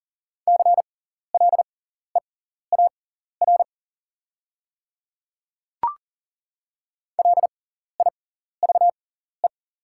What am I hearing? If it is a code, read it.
Morse code, heard as a keyed tone, about 40 wpm, spaced as for 14 wpm, 690 Hz. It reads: CLEAR LIVE
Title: 6x Character Spacing - Top 500 Words - 40wpm